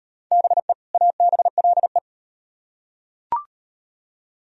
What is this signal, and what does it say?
Morse code, heard as a keyed tone, about 36 wpm, 700 Hz. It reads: BE ABLE